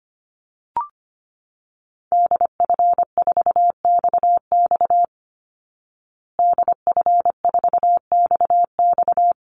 Morse code, heard as a keyed tone, about 25 wpm, 700 Hz. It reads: DF4XX DF4XX
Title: Call Signs Repeated 3x - 25wpm